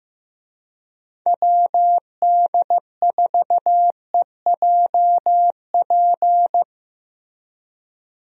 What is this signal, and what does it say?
Morse code, heard as a keyed tone, about 15 wpm, 700 Hz. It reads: WD4EJP
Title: USA Call Signs - 6 Characters - Repeated 3x - 15wpm